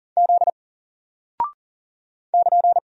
Morse code, heard as a keyed tone, about 39 wpm, 700 Hz. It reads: Z <KN>
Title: Single Character (Letter:Number:Punctuation:Prosign) Alt 40wpm